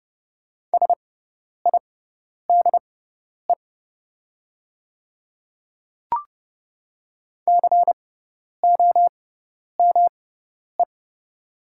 Morse code, heard as a keyed tone, about 30 wpm, spaced as for 10 wpm, 700 Hz. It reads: SIDE COME
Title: 6x Character Spacing - Top 100 Words - 30wpm